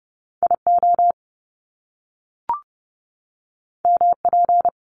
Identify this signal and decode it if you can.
Morse code, heard as a keyed tone, about 30 wpm, 700 Hz. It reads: IO MP
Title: Top Two Letter Combinations in Words 30wpm